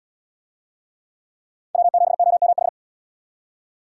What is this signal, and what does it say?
Morse code, heard as a keyed tone, about 77 wpm, 700 Hz. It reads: W7OMH